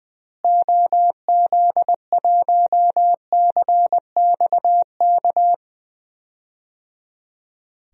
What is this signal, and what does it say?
Morse code, heard as a keyed tone, about 19 wpm, 700 Hz. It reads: OZ1CXK
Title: DXCC-100 Call Signs - 6 Characters - Repeated 3x - 20wpm